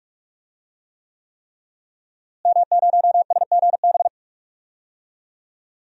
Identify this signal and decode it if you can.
Morse code, heard as a keyed tone, about 41 wpm, 700 Hz. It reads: M0SGB
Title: DXCC-100 Call Signs - 5 Characters - Repeated 3x - 45wpm